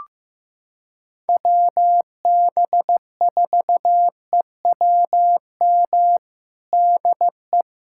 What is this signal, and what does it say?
Morse code, heard as a keyed tone, about 14 wpm, 700 Hz. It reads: WB4EWM DE
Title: USA Callsign DE Callsign 15wpm